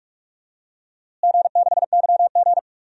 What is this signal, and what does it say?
Morse code, heard as a keyed tone, about 46 wpm, 700 Hz. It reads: G6YG
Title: DXCC-100 Call Signs - 4 Characters - Repeated 2x - 45wpm